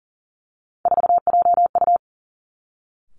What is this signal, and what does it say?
Morse code, heard as a keyed tone, about 42 wpm, 700 Hz. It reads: 4JU
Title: Three Challenging Letter-Number Combinations 40wpm